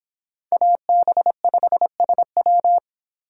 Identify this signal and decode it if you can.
Morse code, heard as a keyed tone, about 26 wpm, 700 Hz. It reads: AB5SW